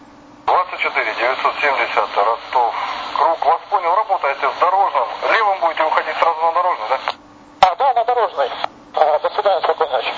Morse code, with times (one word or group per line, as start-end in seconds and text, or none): none